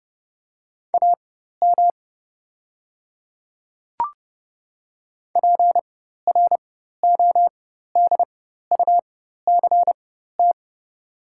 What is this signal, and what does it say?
Morse code, heard as a keyed tone, about 31 wpm, spaced as for 14 wpm, 690 Hz. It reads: AM PRODUCT
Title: 4x Character Spacing - Top 500 Words - 30wpm